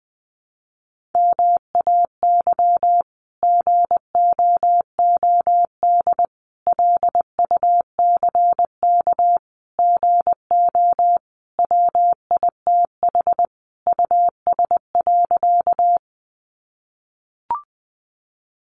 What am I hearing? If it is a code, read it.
Morse code, heard as a keyed tone, about 20 wpm, 700 Hz. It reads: MAY GOOD LUCK GO WITH US.
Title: Sentences from Top 1500 Words 20wpm